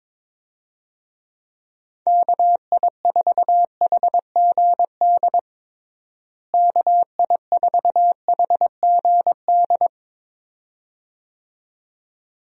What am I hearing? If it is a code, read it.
Morse code, heard as a keyed tone, about 22 wpm, 700 Hz. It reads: KI4HGD KI4HGD
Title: USA Call Signs - 6 Characters - Repeated 2x - 22wpm